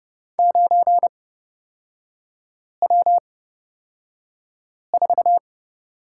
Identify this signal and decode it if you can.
Morse code, heard as a keyed tone, about 30 wpm, spaced as for 5 wpm, 690 Hz. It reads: 9W4
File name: Lesson 37e - Letter x - Three Characters 30wpm